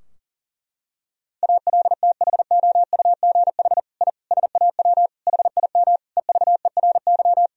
Morse code, heard as a keyed tone, about 37 wpm, 700 Hz. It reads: ALTHOUGH I SAW HIM EVERY
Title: Sentences from Top 1000 Words - Review Part 3 - 40wpm